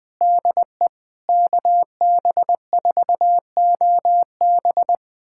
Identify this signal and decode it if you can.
Morse code, heard as a keyed tone, about 20 wpm, 700 Hz. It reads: DE KB4OB
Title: USA Callsign DE Callsign - 5 Characters - 20wpm